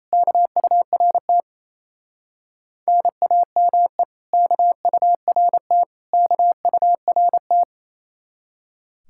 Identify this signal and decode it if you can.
Morse code, heard as a keyed tone, about 35 wpm, 700 Hz. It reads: KURT NAME KURT KURT